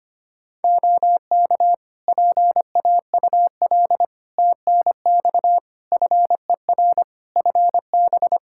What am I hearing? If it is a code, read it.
Morse code, heard as a keyed tone, about 25 wpm, 700 Hz. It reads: OK PAUL TNX FER FB
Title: K1USN SST QSO Practice 25wpm